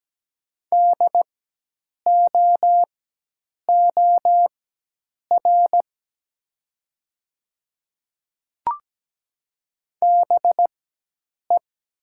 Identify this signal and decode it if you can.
Morse code, heard as a keyed tone, about 17 wpm, spaced as for 8 wpm, 700 Hz. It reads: DOOR BE
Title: 4x Character Spacing - Top 500 Words - 17wpm